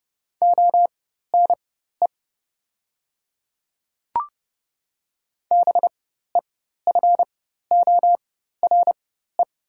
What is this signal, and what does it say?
Morse code, heard as a keyed tone, about 30 wpm, spaced as for 14 wpm, 690 Hz. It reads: ONE BEFORE
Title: 4x Character Spacing - Top 200 Words - 30wpm